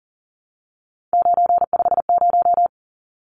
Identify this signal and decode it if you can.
Morse code, heard as a keyed tone, about 40 wpm, 700 Hz. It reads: 950
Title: Numbers - 3-Digit 40wpm